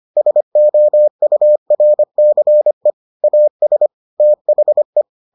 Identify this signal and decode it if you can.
Morse code, heard as a keyed tone, about 25 wpm, 600 Hz. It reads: SOURCE AS THE